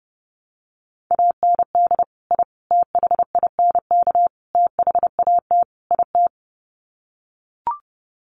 Morse code, heard as a keyed tone, about 30 wpm, 700 Hz. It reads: AND I THINK THAT IT
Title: Sets of 5 Words Encore2 30wpm